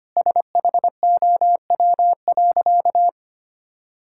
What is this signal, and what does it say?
Morse code, heard as a keyed tone, about 25 wpm, 700 Hz. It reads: SHOW.
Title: Sentences from Top 1000 Words - Review Part 6 - 25wpm